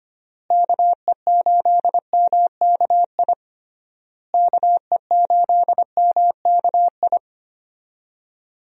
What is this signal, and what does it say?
Morse code, heard as a keyed tone, about 25 wpm, 700 Hz. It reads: KE8MKI KE8MKI
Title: USA Call Signs - 6 Characters - Repeated 3x - 25wpm